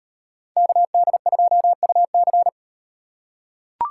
Morse code, heard as a keyed tone, about 38 wpm, 700 Hz. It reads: KD2UC